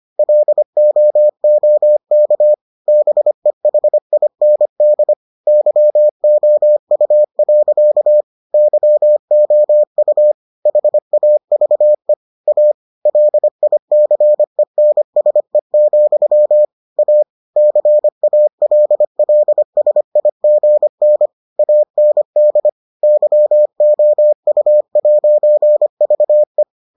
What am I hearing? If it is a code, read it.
Morse code, heard as a keyed tone, about 25 wpm, 600 Hz. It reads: LOOK BEHIND YOU. YOU HAVE A LICENSE, A CALLSIGN AND YOU'VE